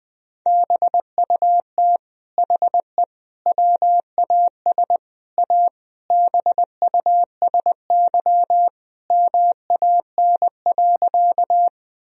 Morse code, heard as a keyed tone, about 20 wpm, 700 Hz. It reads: BUT HE WAS A BUSY MAN.
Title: Sentences from Top 1000 Words - Review Part 3 - 20wpm